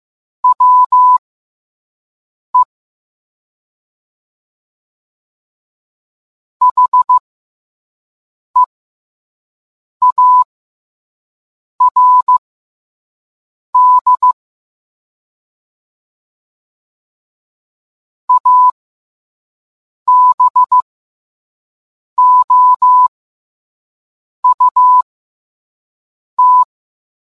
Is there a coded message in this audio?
Yes: Morse code from 0.4 to 26.6 s, heard as a keyed tone, about 15 wpm, spaced as for 5 wpm, 1000 Hz. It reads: WE HEARD ABOUT